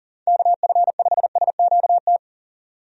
Morse code, heard as a keyed tone, about 40 wpm, 700 Hz. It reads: KF5SQT
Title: USA Call Signs - 6 Characters - 40wpm